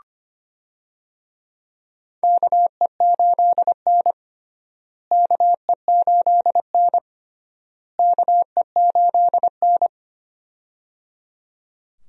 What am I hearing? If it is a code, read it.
Morse code, heard as a keyed tone, about 25 wpm, 700 Hz. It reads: KE8N KE8N KE8N
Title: USA Call Signs - 4 Characters - Repeated 3x - 25wpm